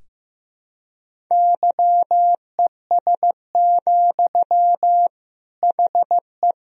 Morse code, heard as a keyed tone, about 15 wpm, 700 Hz. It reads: YES, HE
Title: Sentences from Top 500 Words and Abbreviations 15wpm